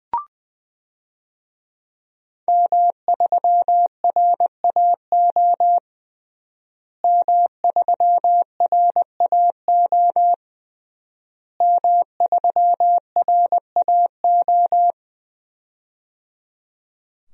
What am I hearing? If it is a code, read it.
Morse code, heard as a keyed tone, about 20 wpm, 700 Hz. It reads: M3RAO M3RAO M3RAO
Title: DXCC-100 Call Signs - 5 Characters - Repeated 3x - 20wpm